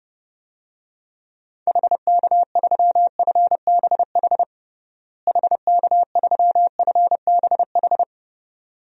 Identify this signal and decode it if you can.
Morse code, heard as a keyed tone, about 30 wpm, 700 Hz. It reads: HK3FBH HK3FBH
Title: DXCC-100 Call Signs - 6 Characters - Repeated 2x - 30wpm